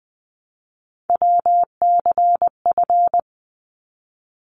Morse code, heard as a keyed tone, about 20 wpm, 700 Hz. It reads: WCF
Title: ARRL Sections 20wpm